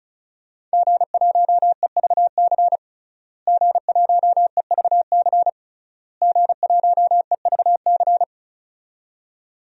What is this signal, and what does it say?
Morse code, heard as a keyed tone, about 34 wpm, 700 Hz. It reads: G1EVC G1EVC G1EVC
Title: DXCC-100 Call Signs - 5 Characters - Repeated 3x - 35wpm